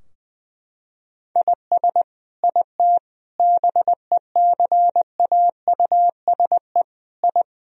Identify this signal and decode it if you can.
Morse code, heard as a keyed tone, about 20 wpm, 700 Hz. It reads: IS IT BECAUSE I
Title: Sentences from Top 1000 Words - Review Part 3 - 20wpm